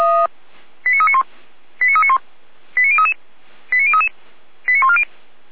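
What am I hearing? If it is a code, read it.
Five-tone selective call (ZVEI), heard as quick runs of five tones, tones of about 70 ms. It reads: 89391 89391 89033 89033 89140
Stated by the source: ZVEI five-tone selective call received on SDR.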